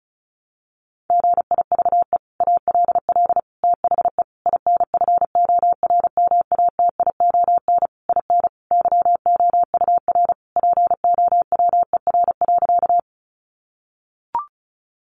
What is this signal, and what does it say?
Morse code, heard as a keyed tone, about 34 wpm, 700 Hz. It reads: GIVE ALL THE INFORMATION IN YOUR POWER.
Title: Sentences from Top 600 Words 35wpm